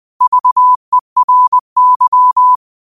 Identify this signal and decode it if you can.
Morse code, heard as a keyed tone, about 20 wpm, 1000 Hz. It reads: VERY